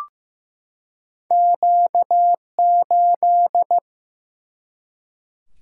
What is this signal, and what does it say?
Morse code, heard as a keyed tone, about 15 wpm, 700 Hz. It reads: Q8
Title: Single-Letter Single-Number 15wpm